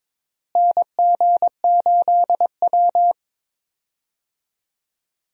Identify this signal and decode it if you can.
Morse code, heard as a keyed tone, about 22 wpm, 700 Hz. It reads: NG8W
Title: USA Call Signs - 4 Characters - Repeated 2x - 22wpm